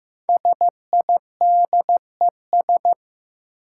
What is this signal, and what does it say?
Morse code, heard as a keyed tone, about 15 wpm, 700 Hz. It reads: SIDES
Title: Sets of 4 Words - New Word Drill - 15wpm